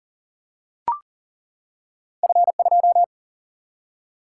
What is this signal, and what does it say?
Morse code, heard as a keyed tone, about 42 wpm, 700 Hz. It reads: F2
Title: Single-Letter Single-Number 40wpm